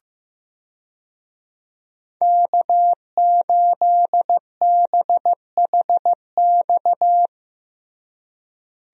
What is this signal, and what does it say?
Morse code, heard as a keyed tone, about 15 wpm, 700 Hz. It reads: K8BHX
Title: USA Call Signs - 5 Characters - Repeated 3x - 15wpm